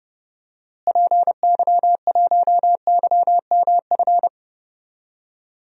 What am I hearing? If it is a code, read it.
Morse code, heard as a keyed tone, about 30 wpm, 700 Hz. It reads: PY1YMF
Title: DXCC-100 Call Signs - 6 Characters - Repeated 2x - 30wpm